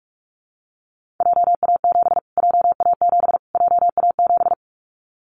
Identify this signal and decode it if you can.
Morse code, heard as a keyed tone, about 44 wpm, 700 Hz. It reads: JA7 JA7 JA7
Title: Callsign Prefixes for Top 25 DXCC - Repeated 3x - 45wpm